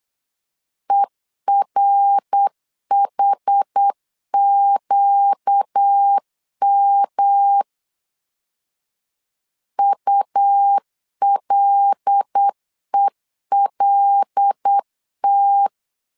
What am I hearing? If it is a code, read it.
Morse code, heard as a keyed tone, about 8 wpm, 800 Hz. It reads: ERHQM ULELT
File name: CW / DET / 1 6 1500C S A 8wpm 5203